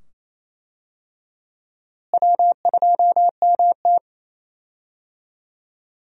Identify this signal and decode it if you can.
Morse code, heard as a keyed tone, about 28 wpm, 700 Hz. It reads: W2MT
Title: USA Call Signs - 4 Characters - Repeated 2x - 28wpm